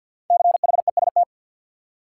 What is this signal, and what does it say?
Morse code, heard as a keyed tone, about 52 wpm, 700 Hz. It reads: CHEST
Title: Five Letter Words - 50wpm